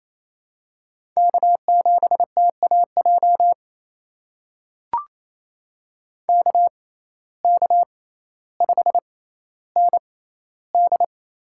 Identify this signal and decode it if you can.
Morse code, heard as a keyed tone, about 28 wpm, spaced as for 14 wpm, 700 Hz. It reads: K7TAJ KK5ND
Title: USA Call Signs - 5 Characters - 6x Character Spacing - 28wpm